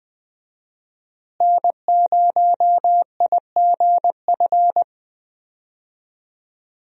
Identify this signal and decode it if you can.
Morse code, heard as a keyed tone, about 20 wpm, 700 Hz. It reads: N0IGF
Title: USA Call Signs - 5 Characters - Repeated 3x - 20wpm